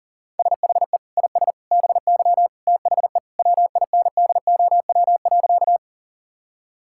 Morse code, heard as a keyed tone, about 40 wpm, 700 Hz. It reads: SHE IS BY THE WINDOW.
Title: Sentences from Top 1600 Words 40wpm